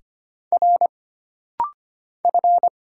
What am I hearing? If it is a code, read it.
Morse code, heard as a keyed tone, about 25 wpm, 700 Hz. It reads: R F
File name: Single Character (Letter:Number:Punctuation:Prosign) - Rapid-Fire - 25wpm